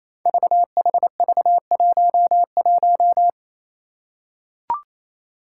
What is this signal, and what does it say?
Morse code, heard as a keyed tone, about 28 wpm, 700 Hz. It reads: VHV11
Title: Five Challenging Letter-Number Combinations 28wpm